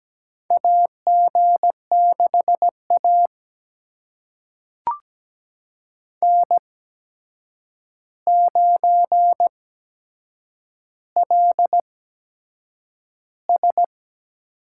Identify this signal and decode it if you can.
Morse code, heard as a keyed tone, about 17 wpm, spaced as for 7 wpm, 700 Hz. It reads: AG6A N9LS
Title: USA Call Signs - 8x Character Spacing - 17wpm